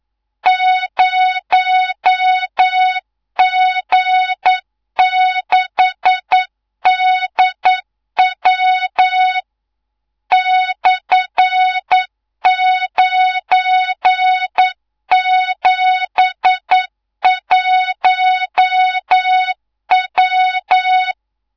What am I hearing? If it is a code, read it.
Morse code, heard as a keyed tone, about 9 wpm, 3700 Hz, with noise. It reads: 0G6DW /971W
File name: Morze-HA2MN / hangp / 2 03 GYAKORLAT